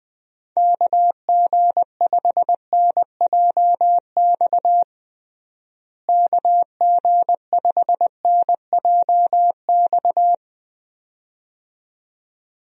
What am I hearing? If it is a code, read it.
Morse code, heard as a keyed tone, about 21 wpm, 700 Hz. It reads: KG5NJX KG5NJX
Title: USA Call Signs - 6 Characters - Repeated 2x - 20wpm